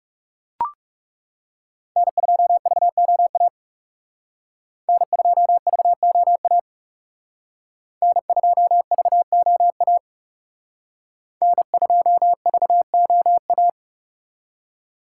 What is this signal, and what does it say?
Morse code, heard as a keyed tone, about 34 wpm, 700 Hz. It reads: N2VOA N2VOA N2VOA N2VOA
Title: Speed Racing - USA 1x3 Call Signs 30wpm